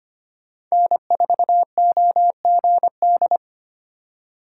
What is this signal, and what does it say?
Morse code, heard as a keyed tone, about 25 wpm, 700 Hz. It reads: N4OGD
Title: USA Call Signs - 5 Characters - 2x Character Spacing - 25wpm